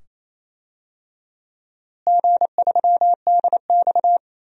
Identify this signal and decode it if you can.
Morse code, heard as a keyed tone, about 28 wpm, 700 Hz. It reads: G3DX
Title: DXCC-100 Call Signs - 4 Characters - Repeated 2x - 28wpm